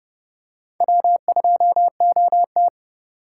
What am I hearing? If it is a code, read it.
Morse code, heard as a keyed tone, about 29 wpm, 700 Hz. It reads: W2OT